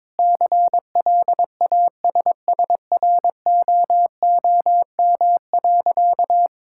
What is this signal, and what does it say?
Morse code, heard as a keyed tone, about 22 wpm, 700 Hz. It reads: CLASSROOM.
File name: Difficult Sentences with Hints - 22wpm